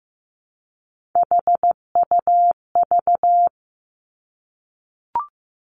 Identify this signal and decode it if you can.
Morse code, heard as a keyed tone, about 15 wpm, 700 Hz. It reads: HUV
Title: Three Challenging Letter-Number Combinations 15wpm